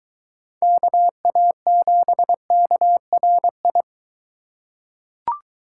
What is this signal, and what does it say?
Morse code, heard as a keyed tone, about 21 wpm, 700 Hz. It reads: KA7KRI